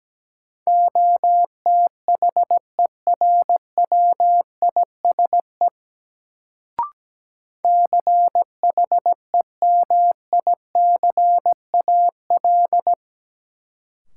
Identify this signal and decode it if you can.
Morse code, heard as a keyed tone, about 17 wpm, 700 Hz. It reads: OTHERWISE CHEMICAL
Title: Sentences from Top 1200 Words - New Word Drill Alt - 17wpm